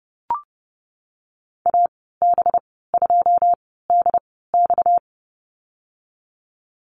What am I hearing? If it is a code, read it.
Morse code, heard as a keyed tone, about 30 wpm, spaced as for 17 wpm, 700 Hz. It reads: AB2DX
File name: Lesson 37g - Letter x - Call Signs 30wpm